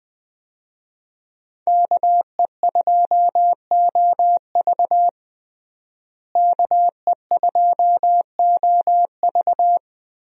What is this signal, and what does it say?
Morse code, heard as a keyed tone, about 21 wpm, 700 Hz. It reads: KE2OV KE2OV
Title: USA Call Signs - 5 Characters - Repeated 3x - 20wpm